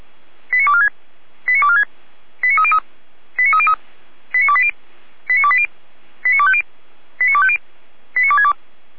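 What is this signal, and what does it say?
Five-tone selective call (ZVEI), heard as quick runs of five tones, tones of about 70 ms. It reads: 89246 89246 89303 89303 89280 89280 89250 89250 89262
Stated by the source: ZVEI five-tone selective call received on an SDR, from feuerwehr-orlamuende, a fire brigade in Orlamünde.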